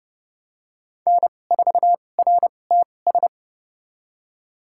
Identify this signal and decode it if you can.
Morse code, heard as a keyed tone, about 30 wpm, spaced as for 22 wpm, 700 Hz. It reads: N4RTS
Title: USA Call Signs - 5 Characters - 2x Character Spacing - 30wpm